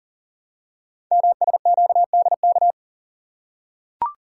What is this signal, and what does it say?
Morse code, heard as a keyed tone, about 38 wpm, 700 Hz. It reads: MSQDK
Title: Code Groups - 5 Letters - 40wpm